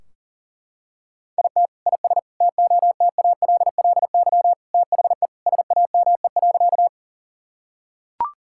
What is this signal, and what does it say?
Morse code, heard as a keyed tone, about 40 wpm, 700 Hz. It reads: IT IS TOTALLY THE SAME.
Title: Sentences from Top 1600 Words 40wpm